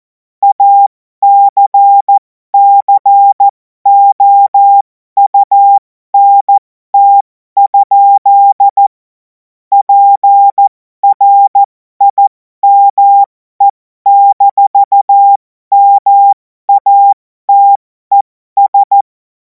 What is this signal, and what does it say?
Morse code, heard as a keyed tone, about 14 wpm, spaced as for 12 wpm, 800 Hz. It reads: ACCOUNT? PRIME-MATES
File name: Morsekurs / Witze